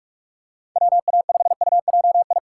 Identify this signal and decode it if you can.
Morse code, heard as a keyed tone, about 43 wpm, 700 Hz. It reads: WA5UJI
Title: USA Call Signs - 6 Characters - 2x Character Spacing - 45wpm